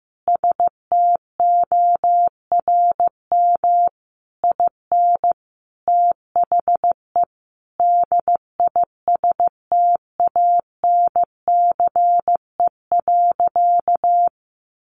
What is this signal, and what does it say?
Morse code, heard as a keyed tone, about 15 wpm, 700 Hz. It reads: STORM IN THE DISTANCE.
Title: Sentences from Top 1500 Words 15wpm